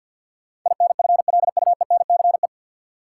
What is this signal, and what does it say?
Morse code, heard as a keyed tone, about 53 wpm, 700 Hz. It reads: INFLUENCE
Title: Sentences from Top 1100 Words - New Word Drill Alt - 50wpm